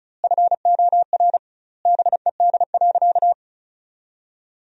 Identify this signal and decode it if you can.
Morse code, heard as a keyed tone, about 36 wpm, 700 Hz. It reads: FOR BED.
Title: 2x Word-Spacing - Sentences from Top 500 Words - 35wpm